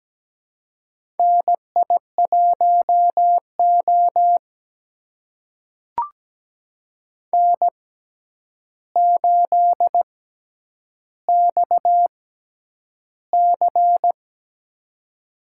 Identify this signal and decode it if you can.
Morse code, heard as a keyed tone, about 17 wpm, spaced as for 9 wpm, 700 Hz. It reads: NI1O N8XC
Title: USA Call Signs - 4 Characters - 6x Character Spacing - 17wpm